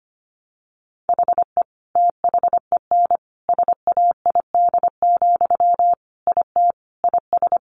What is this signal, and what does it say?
Morse code, heard as a keyed tone, about 25 wpm, 700 Hz. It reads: HE THEN SAID, IT IS
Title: Sentences from Top 300 Words 25wpm